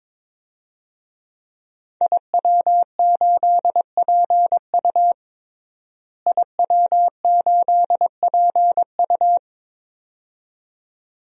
Morse code, heard as a keyed tone, about 22 wpm, 700 Hz. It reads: IW8PU IW8PU
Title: DXCC-100 Call Signs - 5 Characters - Repeated 2x - 22wpm